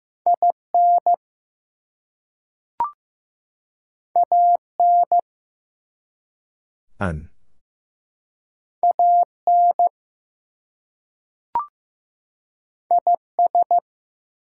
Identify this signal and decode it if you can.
Morse code, heard as a keyed tone, about 15 wpm, 700 Hz. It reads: IN AN AN IS
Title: Two Letter Words - 15wpm